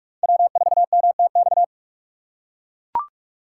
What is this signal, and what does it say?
Morse code, heard as a keyed tone, about 43 wpm, 700 Hz. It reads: W4MTX